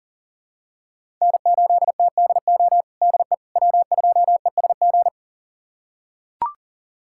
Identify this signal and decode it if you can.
Morse code, heard as a keyed tone, about 40 wpm, 700 Hz. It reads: N8TDO DE W2ESG
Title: USA Callsign DE Callsign - 5 Characters - 40wpm